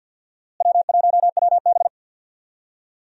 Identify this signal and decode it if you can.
Morse code, heard as a keyed tone, about 52 wpm, 700 Hz. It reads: W1WB
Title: USA Call Signs - 4 Characters - Repeated 3x - 50wpm